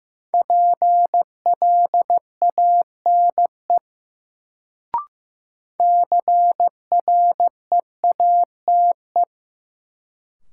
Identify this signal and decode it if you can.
Morse code, heard as a keyed tone, about 16 wpm, 700 Hz. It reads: PLANE CREATE
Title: Sentences from Top 1000 Words - Review Part 3 - New Word Drill - 15wpm